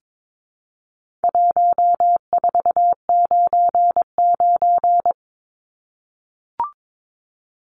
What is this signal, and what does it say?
Morse code, heard as a keyed tone, about 22 wpm, 700 Hz. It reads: 1499
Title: Numbers - 4-Digit 22wpm